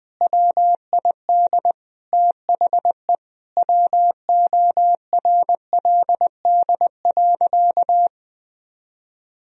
Morse code, heard as a keyed tone, about 20 wpm, 700 Hz. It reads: WID THE WORLD.
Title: Sentences from Top 200 Words and Abbreviations 20wpm